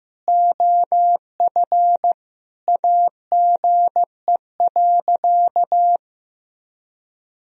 Morse code, heard as a keyed tone, about 15 wpm, 700 Hz. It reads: OF AGE.
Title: Sentences from Top 1000 Words - Review Part 3 - 15wpm